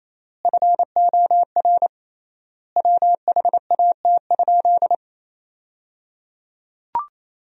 Morse code, heard as a keyed tone, about 28 wpm, 700 Hz. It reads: FOR WHAT?
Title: 3x Word-Spacing - Sentences from Top 100 Words - 28wpm